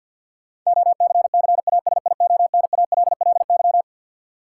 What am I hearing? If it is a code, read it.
Morse code, heard as a keyed tone, about 52 wpm, 700 Hz. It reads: OCCASIONALLY
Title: Sentences from Top 1300 Words - New Word Drill Alt - 50wpm